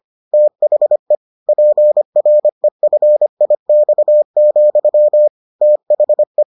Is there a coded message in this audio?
Yes: Morse code from 0.3 to 6.4 s, heard as a keyed tone, about 25 wpm, 600 Hz. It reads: THE PREFIX, THE